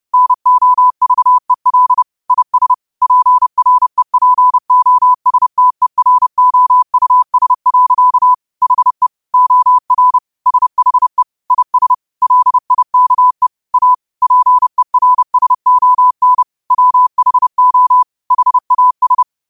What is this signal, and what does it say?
Morse code, heard as a keyed tone, about 30 wpm, 1000 Hz. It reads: NOVEL IS PREPOSTEROUS. HE OR SHE IS LIKE A PERSON WHO HAS
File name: qotdfiles / 30wpm